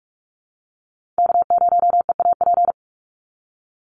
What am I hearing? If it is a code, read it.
Morse code, heard as a keyed tone, about 44 wpm, 700 Hz. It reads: K0EAP
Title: NA SOTA Chaser Callsigns 45wpm